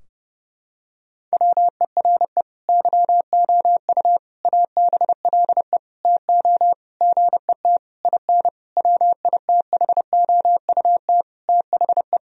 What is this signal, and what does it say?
Morse code, heard as a keyed tone, about 30 wpm, 700 Hz. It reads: WERE YOU ABLE TO GET IN WITHOUT THE
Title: Sentences from Top 1000 Words - Review Part 4 - 30wpm